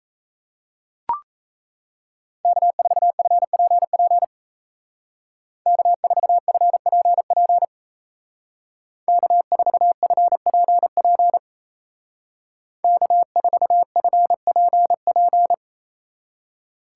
Morse code, heard as a keyed tone, about 35 wpm, 700 Hz. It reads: K4FPP K4FPP K4FPP K4FPP
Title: Speed Racing - USA Call Signs 28wpm